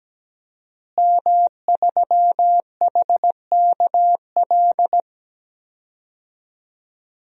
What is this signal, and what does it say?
Morse code, heard as a keyed tone, about 17 wpm, 700 Hz. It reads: M3HKL
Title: DXCC-100 Call Signs - 5 Characters - Repeated 2x - 17wpm